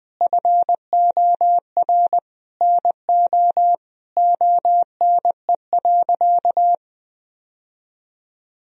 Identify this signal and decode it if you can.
Morse code, heard as a keyed tone, about 21 wpm, 700 Hz. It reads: FOR NO ONE.
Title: Sentences from Top 1000 Words - Review Part 2 - 20wpm